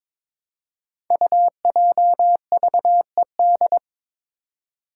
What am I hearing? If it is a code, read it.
Morse code, heard as a keyed tone, about 22 wpm, 700 Hz. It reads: UJVED
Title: Five Challenging Letter-Number Combinations 22wpm